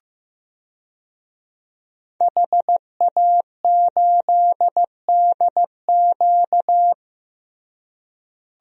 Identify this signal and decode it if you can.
Morse code, heard as a keyed tone, about 15 wpm, 700 Hz. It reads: HA8DQ
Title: DXCC-100 Call Signs - 5 Characters - Repeated 3x - 15wpm